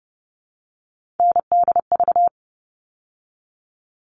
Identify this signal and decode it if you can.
Morse code, heard as a keyed tone, about 30 wpm, 700 Hz. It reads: NDV
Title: All Cut Numbers - 3-Digit 30wpm